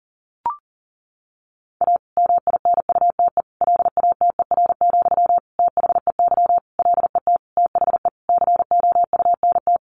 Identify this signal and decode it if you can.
Morse code, heard as a keyed tone, about 37 wpm, 700 Hz. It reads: A MINUTE LATER, THEY LET THE COUNT
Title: Sentences from Top 1500 Words 40wpm